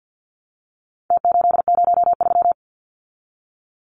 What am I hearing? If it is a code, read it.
Morse code, heard as a keyed tone, about 50 wpm, 700 Hz. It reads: T803
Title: Numbers - Contest Style - 4-Digit 50wpm